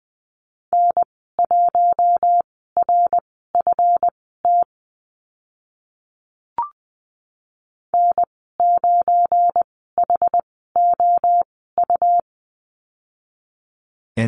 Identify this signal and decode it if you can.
Morse code, heard as a keyed tone, about 20 wpm, spaced as for 15 wpm, 700 Hz. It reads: N1RFT N9HOU
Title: 193 Lesson 31g - Letter v - Call Signs 20wpm